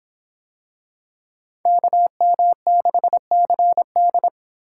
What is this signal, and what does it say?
Morse code, heard as a keyed tone, about 26 wpm, 700 Hz. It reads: KM6CD